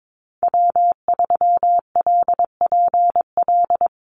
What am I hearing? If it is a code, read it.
Morse code, heard as a keyed tone, about 22 wpm, 700 Hz. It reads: W3LPL
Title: Call Signs Repeated 3x - 22wpm